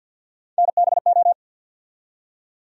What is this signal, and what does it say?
Morse code, heard as a keyed tone, about 48 wpm, 700 Hz. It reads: NBO
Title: USA Call Sign Postfixes - 3 Letters 50wpm